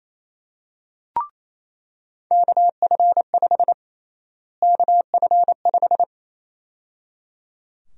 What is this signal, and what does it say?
Morse code, heard as a keyed tone, about 28 wpm, 700 Hz. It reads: KF5 KF5
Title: USA Call Sign Prefixes - Repeated 2x - 28wpm